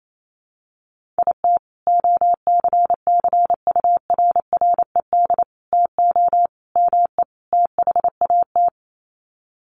Morse code, heard as a keyed tone, about 28 wpm, 700 Hz. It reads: IT OCCURRED TO ME THAT
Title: Sets of 5 Words Encore 28wpm